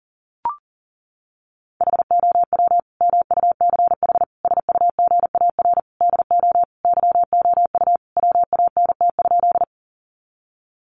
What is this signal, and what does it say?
Morse code, heard as a keyed tone, about 37 wpm, 700 Hz. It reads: HOW MUCH SUGAR DO YOU WANT?
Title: Sentences from Top 1200 Words 40wpm